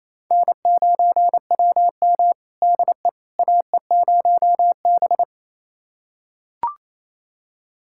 Morse code, heard as a keyed tone, about 28 wpm, 700 Hz. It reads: N9WM DE AE0B
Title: USA Callsign DE Callsign - 4 Characters - 28wpm